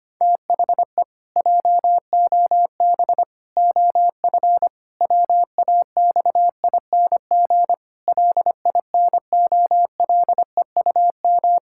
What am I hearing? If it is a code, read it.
Morse code, heard as a keyed tone, about 25 wpm, 700 Hz. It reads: THE JOB OF WAXING LINOLEUM
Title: Pangram Challenge v2 - 25wpm